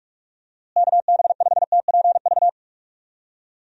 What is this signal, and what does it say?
Morse code, heard as a keyed tone, about 44 wpm, 700 Hz. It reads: KB5TPV
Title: USA Call Signs - 6 Characters - Repeated 3x - 45wpm